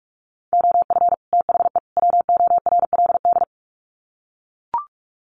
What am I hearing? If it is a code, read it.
Morse code, heard as a keyed tone, about 44 wpm, 700 Hz. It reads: OF THE WORLD
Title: Sets of 3 Words 45wpm